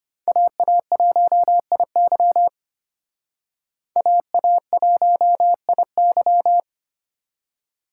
Morse code, heard as a keyed tone, about 33 wpm, 700 Hz. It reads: AA1IY AA1IY